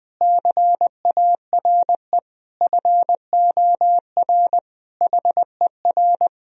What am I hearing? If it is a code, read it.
Morse code, heard as a keyed tone, about 21 wpm, 700 Hz. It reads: CARE FOR HER
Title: Sentences from Top 1000 Words - Review Part 3 - 20wpm